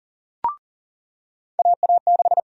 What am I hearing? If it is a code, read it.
Morse code, heard as a keyed tone, about 40 wpm, 700 Hz. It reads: AA6
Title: USA Call Sign Prefixes - 40wpm